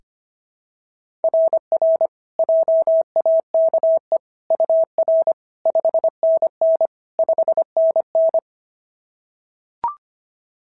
Morse code, heard as a keyed tone, about 23 wpm, 650 Hz. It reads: RR JAKE UR 5NN 5NN
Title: SOTA QSO Practice 25wpm